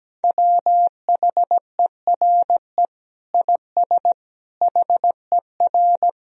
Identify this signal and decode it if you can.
Morse code, heard as a keyed tone, about 17 wpm, 700 Hz. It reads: WHERE IS HER
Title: Sentences from Top 1000 Words - Review Part 2 - 17wpm